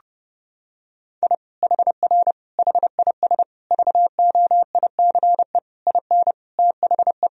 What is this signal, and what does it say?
Morse code, heard as a keyed tone, about 32 wpm, 700 Hz. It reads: I HR HIS VOICE IN THE
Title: Sentences from Top 500 Words and Abbreviations 30wpm